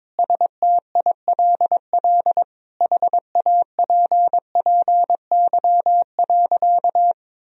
Morse code, heard as a keyed tone, about 22 wpm, 700 Hz. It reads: STILL HAPPY.
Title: Sentences from Top 1000 Words - Review Part 3 - 22wpm